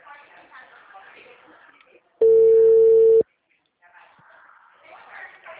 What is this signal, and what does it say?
Ringback tone (China, 450 Hz), with speech.
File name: out- -2006- - - .694